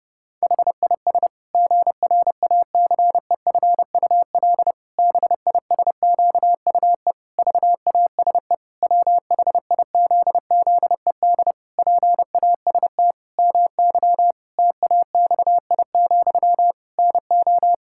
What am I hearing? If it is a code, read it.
Morse code, heard as a keyed tone, about 30 wpm, 700 Hz. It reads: HIS GRACEFUL BISQUE VASE WHIZZED PAST MY TAXI, NO
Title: Pangram Challenge v2 - 30wpm